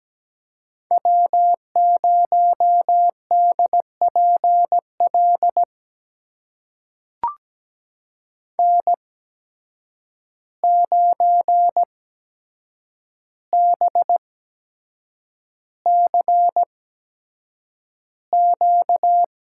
Morse code, heard as a keyed tone, about 17 wpm, spaced as for 7 wpm, 700 Hz. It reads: W0DPL N9BCQ
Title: USA Call Signs - 8x Character Spacing - 17wpm